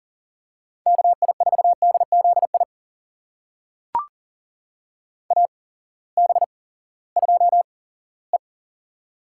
Morse code, heard as a keyed tone, about 40 wpm, spaced as for 20 wpm, 700 Hz. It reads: KI4DZI AB2E
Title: USA Call Signs - 8x Character Spacing - 40wpm